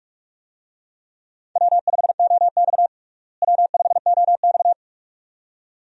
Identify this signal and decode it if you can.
Morse code, heard as a keyed tone, about 46 wpm, 700 Hz. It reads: W5OX W5OX
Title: USA Call Signs - 4 Characters - Repeated 2x - 45wpm